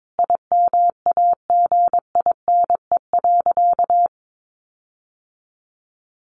Morse code, heard as a keyed tone, about 22 wpm, 700 Hz. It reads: IMAGINE.
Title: Sentences from Top 1400 Words 22wpm